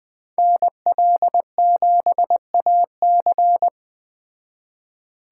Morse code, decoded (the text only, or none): NL7AC